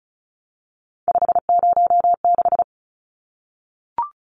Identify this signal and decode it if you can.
Morse code, heard as a keyed tone, about 34 wpm, 700 Hz. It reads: 506
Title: Numbers - 3-Digit 35wpm